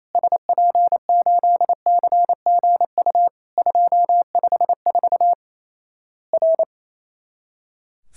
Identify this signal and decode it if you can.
Morse code, heard as a keyed tone, about 28 wpm, 700 Hz. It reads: SP8CGU 254 R
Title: Worked All Europe DX Contest - QTC Exchange - 28wpm